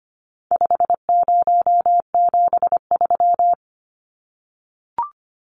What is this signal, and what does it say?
Morse code, heard as a keyed tone, about 25 wpm, 700 Hz. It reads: 5073
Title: Numbers - 4-Digit 25wpm